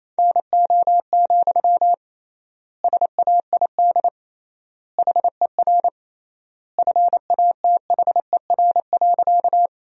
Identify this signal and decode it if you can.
Morse code, heard as a keyed tone, about 28 wpm, 700 Hz. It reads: NO, SAID HER FATHER.
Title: 3x Word-Spacing - Sentences from Top 300 Words - 28wpm